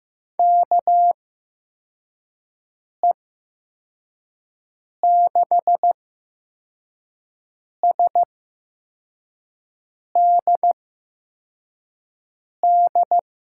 Morse code, heard as a keyed tone, about 15 wpm, spaced as for 4 wpm, 700 Hz. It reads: KE6SDD
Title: USA Call Signs - 6 Characters - 8x Character Spacing - 15wpm